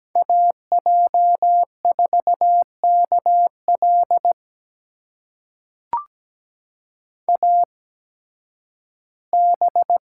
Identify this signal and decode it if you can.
Morse code, heard as a keyed tone, about 17 wpm, spaced as for 11 wpm, 700 Hz. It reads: AJ4KL AB
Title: USA Call Signs - 5 Characters - 8x Character Spacing - 17wpm